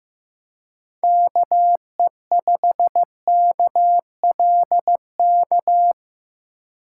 Morse code, heard as a keyed tone, about 18 wpm, 700 Hz. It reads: KE5KLK
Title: Speed Racing - USA 2x3 Call Signs 15wpm